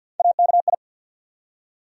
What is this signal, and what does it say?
Morse code, heard as a keyed tone, about 48 wpm, 700 Hz. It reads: AKI
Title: USA Call Sign Postfixes - 3 Letters 50wpm